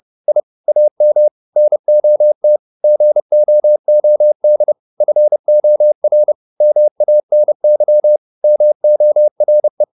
Morse code, heard as a keyed tone, about 30 wpm, 600 Hz. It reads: I AM NOT GOOD FOR MANY MORE